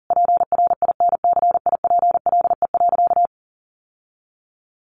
Morse code, heard as a keyed tone, about 40 wpm, 700 Hz. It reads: PRINCIPLE.
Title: Sentences from Top 1400 Words 40wpm